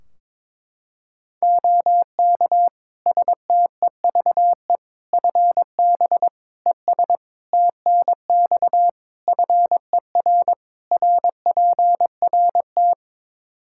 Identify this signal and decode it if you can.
Morse code, heard as a keyed tone, about 22 wpm, 700 Hz. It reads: OK STEVE FB ES TNX FER RPRT
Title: Practice QSO - Exchange 2 - Info - 22wpm